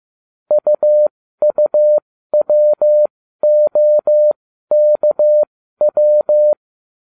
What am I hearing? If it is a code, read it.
Morse code, heard as a keyed tone, about 15 wpm, spaced as for 12 wpm, 600 Hz. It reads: UUWOKW